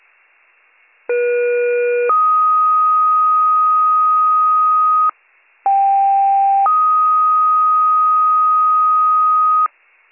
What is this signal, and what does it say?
Two-tone page, heard as a one-second tone, then a three-second tone, about 480 then 1250 Hz, about 790 then 1250 Hz. It is heard from 1.1 to 9.7 s.